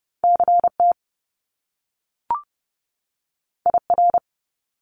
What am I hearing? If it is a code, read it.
Morse code, heard as a keyed tone, about 30 wpm, 700 Hz. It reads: CT IR
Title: Top Two Letter Combinations in Words 30wpm